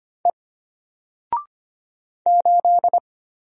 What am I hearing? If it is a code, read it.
Morse code, heard as a keyed tone, about 25 wpm, 700 Hz. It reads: E 8
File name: Single-Letter-Number-25wpm